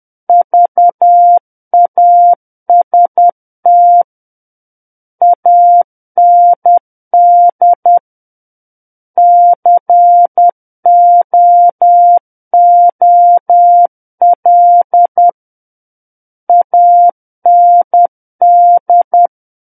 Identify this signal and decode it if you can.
Morse code, heard as a keyed tone, about 10 wpm, 700 Hz. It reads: VAST AND COOL AND
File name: Book One / 01-Chapter 1 - 10 WPM